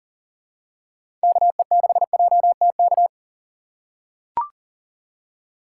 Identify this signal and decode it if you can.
Morse code, heard as a keyed tone, about 40 wpm, 700 Hz. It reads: KE6JTK
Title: USA Call Signs - 6 Characters - 3x Character Spacing - 40wpm